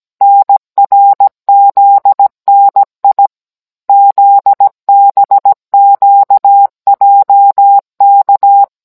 Morse code, heard as a keyed tone, about 17 wpm, 800 Hz. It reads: NRZNI ZBQJK